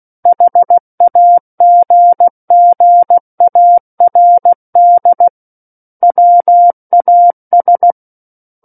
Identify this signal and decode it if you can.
Morse code, heard as a keyed tone, about 16 wpm, 700 Hz. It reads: HAGGARD WAS
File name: Book One / 07-Chapter 7 - 16 WPM